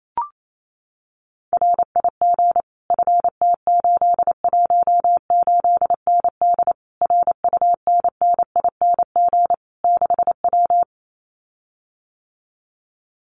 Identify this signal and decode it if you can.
Morse code, heard as a keyed tone, about 28 wpm, 700 Hz. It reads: RIG FT818ND RUNNING 6W
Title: Common QSO Phrases Encore 28wpm